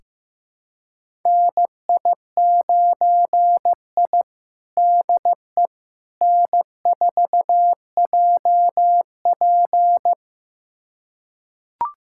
Morse code, heard as a keyed tone, about 15 wpm, 700 Hz. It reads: NI9I DE N4JP
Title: USA Callsign DE Callsign - 4 Characters - 15wpm